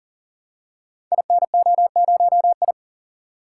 Morse code, heard as a keyed tone, about 40 wpm, 690 Hz. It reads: INO0I
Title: Code Groups - 5 Letter:Numbers - 40wpm